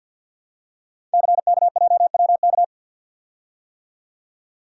Morse code, heard as a keyed tone, about 48 wpm, 700 Hz. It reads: CKJWK
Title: Code Groups - 5 Letters - 50wpm